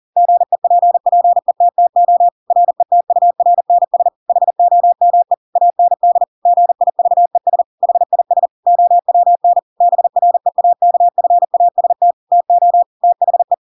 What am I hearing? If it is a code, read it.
Morse code, heard as a keyed tone, about 40 wpm, 700 Hz. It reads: GEPPETTO RETURNS HOME AND GIVES HIS OWN BREAKFAST TO THE
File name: Pinocchio / Pinocchio ch7 40 40 ik1hge